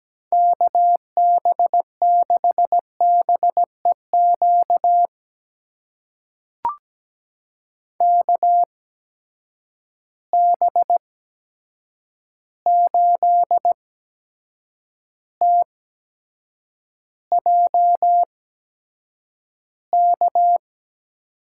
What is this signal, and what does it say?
Morse code, heard as a keyed tone, about 17 wpm, spaced as for 7 wpm, 700 Hz. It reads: KB6BEQ KB8TJK
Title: USA Call Signs - 6 Characters - 8x Character Spacing - 17wpm